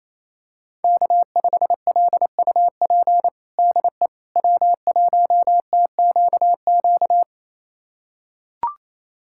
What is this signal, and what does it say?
Morse code, heard as a keyed tone, about 27 wpm, 700 Hz. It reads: K5LUP DE W1TQQ
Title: USA Callsign DE Callsign - 5 Characters - 28wpm